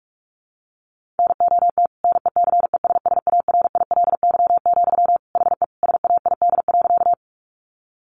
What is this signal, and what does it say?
Morse code, heard as a keyed tone, about 44 wpm, 700 Hz. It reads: NOT NECESSARILY, HE SAID.
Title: Sentences from Top 1400 Words 45wpm